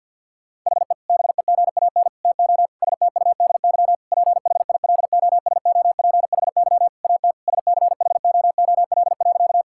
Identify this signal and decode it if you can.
Morse code, heard as a keyed tone, about 50 wpm, 700 Hz. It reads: HE BEGAN TO STUDY PHILOSOPHY AT SCHOOL.
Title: Sentences from Top 1000 Words - Review Part 6 - 50wpm